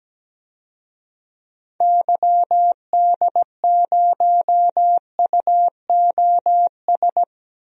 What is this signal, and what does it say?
Morse code, heard as a keyed tone, about 17 wpm, 700 Hz. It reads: YD0UOS
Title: DXCC-100 Call Signs - 6 Characters - Repeated 2x - 17wpm